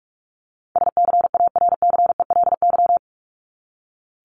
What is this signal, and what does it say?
Morse code, heard as a keyed tone, about 44 wpm, 700 Hz. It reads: SCARCELY
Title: LY Suffix - Top 100 Words with Base Words 45wpm